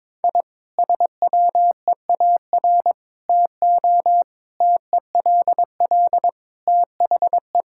Morse code, heard as a keyed tone, about 22 wpm, 700 Hz. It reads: I SWEAR TO TELL THE
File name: Sentences with 9-Word Phrases - 22wpm